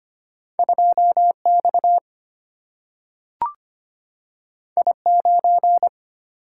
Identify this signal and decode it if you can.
Morse code, heard as a keyed tone, about 25 wpm, 700 Hz. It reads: 2 X I 9
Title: Single-Letter Single-Number 25wpm